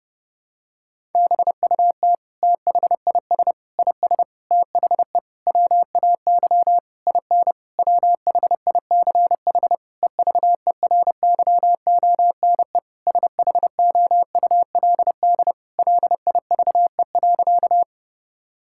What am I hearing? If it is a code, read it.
Morse code, heard as a keyed tone, about 30 wpm, 700 Hz. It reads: BUT THIS IS THE WAY IN WHICH EVERYONE SHOULD LIVE.
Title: Sentences from Top 1000 Words - Review Part 5 - 30wpm